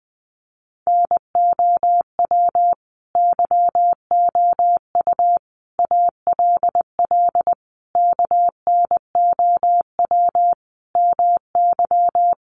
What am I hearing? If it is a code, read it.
Morse code, heard as a keyed tone, about 20 wpm, 700 Hz. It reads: NOW YOU ALL KNOW MY
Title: Sentences from Top 1200 Words 20wpm